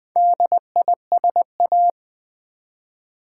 Morse code, heard as a keyed tone, about 20 wpm, 700 Hz. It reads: DISA
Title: Build A Word - 4 Letters At A Time - 12-Letter Words - 20wpm